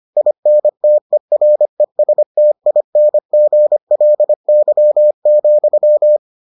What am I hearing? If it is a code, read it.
Morse code, heard as a keyed tone, about 25 wpm, 600 Hz. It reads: INTERESTINGLY,